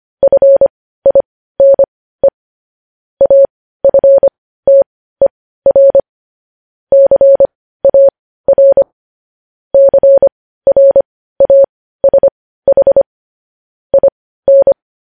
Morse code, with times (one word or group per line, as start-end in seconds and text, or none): 0.2-2.3 FINE
3.2-6.0 AFTER
6.9-8.8 CAR
9.7-13.0 CRASH
13.9-14.7 IN